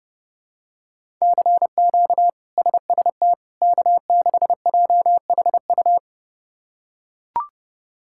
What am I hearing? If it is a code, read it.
Morse code, heard as a keyed tone, about 30 wpm, 700 Hz. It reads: CQ SST K6JHU
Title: K1USN SST QSO Practice - 30wpm